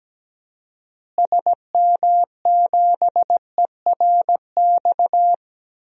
Morse code, heard as a keyed tone, about 17 wpm, 700 Hz. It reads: SM7ERX
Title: DXCC-100 Call Signs - 6 Characters - Repeated 3x - 17wpm